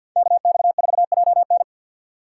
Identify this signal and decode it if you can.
Morse code, heard as a keyed tone, about 51 wpm, 700 Hz. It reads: KX4JN
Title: USA Call Signs - 4x Character Spacing - 50wpm